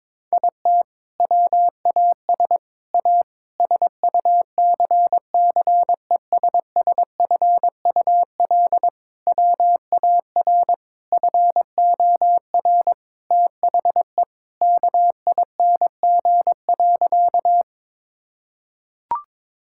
Morse code, heard as a keyed tone, about 22 wpm, 700 Hz. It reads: IT WAS A SUCCESSFUL WAR FOR THE KING.
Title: Sentences from Top 1000 Words - Review Part 6 - 22wpm